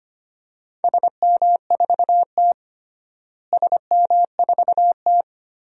Morse code, heard as a keyed tone, about 25 wpm, 700 Hz. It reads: SM4T SM4T
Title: DXCC-100 Call Signs - 4 Characters - Repeated 3x - 25wpm